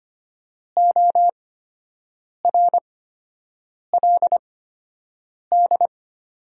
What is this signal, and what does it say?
Morse code, heard as a keyed tone, about 25 wpm, spaced as for 7 wpm, 700 Hz. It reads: ORLD
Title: 8x Character Spacing - Top 300 Words - 25wpm